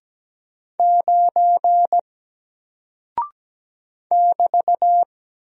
Morse code, heard as a keyed tone, about 17 wpm, 700 Hz. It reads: 9 <BT>
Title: Single Character (Letter:Number:Punctuation:Prosign) 17wpm